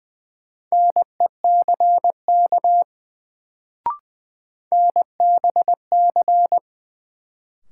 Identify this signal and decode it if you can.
Morse code, heard as a keyed tone, about 20 wpm, 700 Hz. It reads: NECK NBC
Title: Sets of 5 Words Encore - New Word Drill - 20wpm